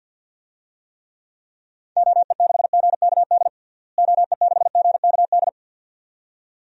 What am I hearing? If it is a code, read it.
Morse code, heard as a keyed tone, about 50 wpm, 700 Hz. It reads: OE6GKD OE6GKD
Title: DXCC-100 Call Signs - 6 Characters - Repeated 2x - 50wpm